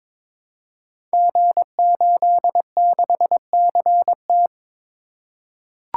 Morse code, heard as a keyed tone, about 22 wpm, 700 Hz. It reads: G86CT
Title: Code Groups - 5 Letter:Numbers - 22wpm